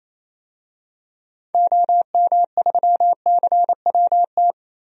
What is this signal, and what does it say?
Morse code, heard as a keyed tone, about 28 wpm, 700 Hz. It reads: OM3CWT
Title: DXCC-100 Call Signs - 6 Characters - Repeated 2x - 28wpm